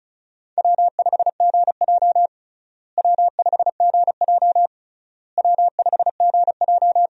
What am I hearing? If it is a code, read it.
Morse code, heard as a keyed tone, about 34 wpm, 700 Hz. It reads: W5GJ W5GJ W5GJ
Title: USA Call Signs - 4 Characters - Repeated 3x - 35wpm